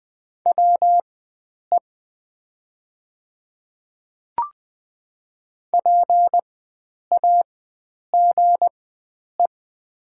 Morse code, heard as a keyed tone, about 20 wpm, spaced as for 9 wpm, 700 Hz. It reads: WE PAGE
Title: 4x Character Spacing - Top 500 Words - 20wpm